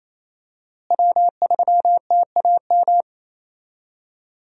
Morse code, heard as a keyed tone, about 27 wpm, 700 Hz. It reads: W3TAM
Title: USA Call Signs - 5 Characters - Repeated 3x - 28wpm